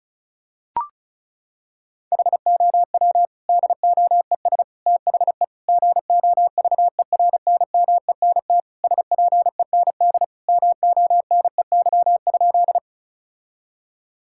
Morse code, heard as a keyed tone, about 34 wpm, 700 Hz. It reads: HOW DOES THE GOVERNMENT SPEND MONEY?
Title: Sentences from Top 1000 Words - Review Part 5 - 35wpm